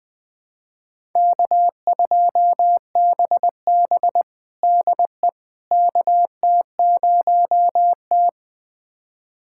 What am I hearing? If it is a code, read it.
Morse code, heard as a keyed tone, about 21 wpm, 700 Hz. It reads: K2BB DE KT0T
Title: USA Callsign DE Callsign - 4 Characters - 20wpm